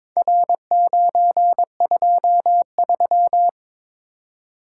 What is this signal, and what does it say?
Morse code, heard as a keyed tone, about 22 wpm, 700 Hz. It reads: R923
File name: practice / 22wpm / Four Letter-Number Combinations - 22wpm